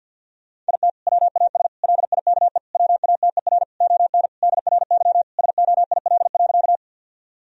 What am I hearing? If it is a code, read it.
Morse code, heard as a keyed tone, about 50 wpm, 700 Hz. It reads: IT WAS LIKE WATER ON DRY SOIL.
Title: Sentences from Top 1000 Words - Review Part 4 - 50wpm